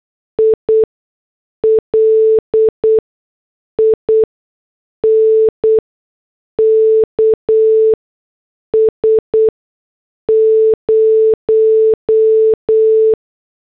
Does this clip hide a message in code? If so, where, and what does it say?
Yes: Morse code from 0.4 to 13.1 s, heard as a keyed tone, about 8 wpm, spaced as for 6 wpm, 440 Hz. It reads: I LINKS 0